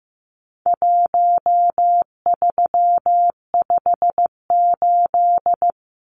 Morse code, heard as a keyed tone, about 15 wpm, 700 Hz. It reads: 1358
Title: Numbers - Contest Style - 4-Digit 15wpm